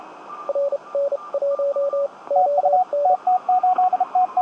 Morse code, heard as a keyed tone, about 31 wpm, spaced as for 25 wpm, 560 Hz, with radio noise. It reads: RN1ON
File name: tt8tt-10m-cw-here-in-jn90ab